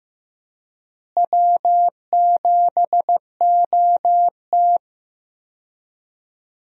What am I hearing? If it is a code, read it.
Morse code, heard as a keyed tone, about 16 wpm, 700 Hz. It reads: W7OT
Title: USA Call Signs - 4 Characters - Repeated 2x - 15wpm